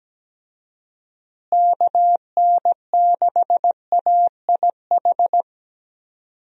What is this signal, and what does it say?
Morse code, heard as a keyed tone, about 17 wpm, 700 Hz. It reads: KN6AIH
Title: USA Call Signs - 6 Characters - Repeated 2x - 17wpm